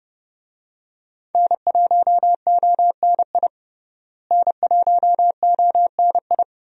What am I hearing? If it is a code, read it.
Morse code, heard as a keyed tone, about 30 wpm, 700 Hz. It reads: N1ONI N1ONI
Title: USA Call Signs - 5 Characters - Repeated 3x - 30wpm